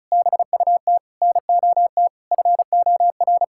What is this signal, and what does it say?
Morse code, heard as a keyed tone, about 34 wpm, 700 Hz. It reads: BUT NOT FOR
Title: Sentences from Top 1600 Words 35wpm